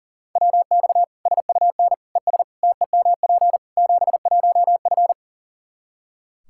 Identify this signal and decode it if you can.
Morse code, heard as a keyed tone, about 40 wpm, 700 Hz. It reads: WX SUN ES TEMP 71F
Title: Practice QSO - Exchange 2 - Info - 40wpm